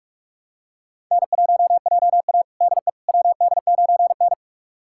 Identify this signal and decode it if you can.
Morse code, heard as a keyed tone, about 46 wpm, 700 Hz. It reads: N1JA DE WD9N
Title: USA Callsign DE Callsign - 4 Characters - 45wpm